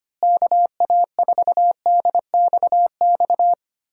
Morse code, heard as a keyed tone, about 25 wpm, 700 Hz. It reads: KA4DXX
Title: USA Call Signs - 6 Characters - Repeated 3x - 25wpm